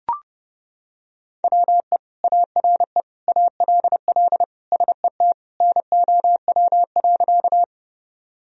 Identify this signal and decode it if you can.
Morse code, heard as a keyed tone, about 30 wpm, 700 Hz. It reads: WE ARE ALL SET NOW.
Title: Sentences from Top 1000 Words - Review Part 1 - 30wpm